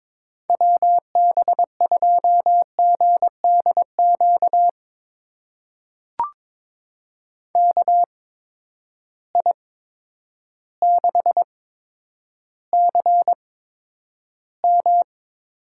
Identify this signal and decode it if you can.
Morse code, heard as a keyed tone, about 22 wpm, spaced as for 10 wpm, 700 Hz. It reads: WB2GDQ KI6CM
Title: USA Call Signs - 6 Characters - 8x Character Spacing - 22wpm